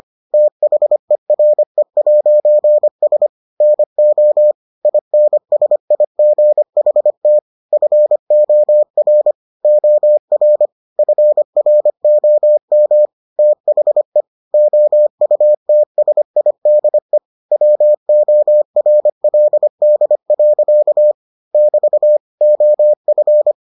Morse code, heard as a keyed tone, about 25 wpm, 600 Hz. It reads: THERE'S NO INSIGHT FOR OR FROM THE OUTSIDE WORLD. <BT> OF